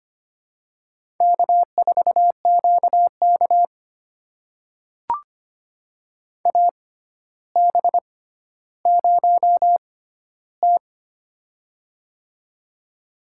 Morse code, heard as a keyed tone, about 25 wpm, spaced as for 13 wpm, 700 Hz. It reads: K4QK AB0T
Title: USA Call Signs - 4 Characters - 6x Character Spacing - 25wpm